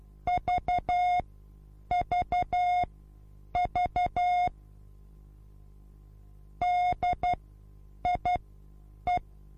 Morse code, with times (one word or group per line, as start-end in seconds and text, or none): 0.3-4.5 VVV
6.6-9.2 DIE